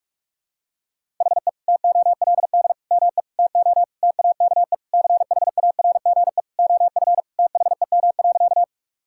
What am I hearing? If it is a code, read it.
Morse code, heard as a keyed tone, about 46 wpm, 700 Hz. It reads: HE TOLD ME TO TAKE CHARGE OF THEM.
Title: Sentences from Top 1000 Words - Review Part 6 - 45wpm